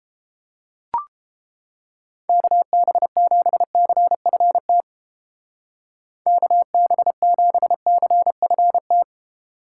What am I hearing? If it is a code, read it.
Morse code, heard as a keyed tone, about 29 wpm, 700 Hz. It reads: KB7CFT KB7CFT